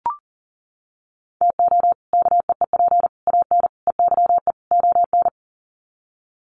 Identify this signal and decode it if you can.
Morse code, heard as a keyed tone, about 40 wpm, 700 Hz. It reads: TO KEEP AN EYE ON
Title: Sets of 5 Words Encore 40wpm